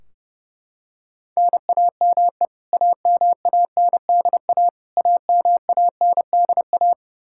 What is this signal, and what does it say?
Morse code, heard as a keyed tone, about 32 wpm, 700 Hz. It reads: NAME AMANDA AMANDA
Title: K1USN SST QSO Practice 30wpm